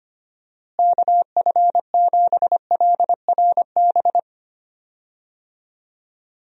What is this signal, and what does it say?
Morse code, heard as a keyed tone, about 25 wpm, 700 Hz. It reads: KF7LRB
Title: USA Call Signs - 6 Characters - Repeated 2x - 25wpm